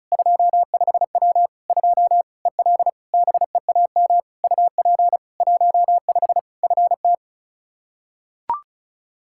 Morse code, heard as a keyed tone, about 34 wpm, 700 Hz. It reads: 25W 2 EL BEAM UP 15 FT
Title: K1USN SST QSO Practice 35wpm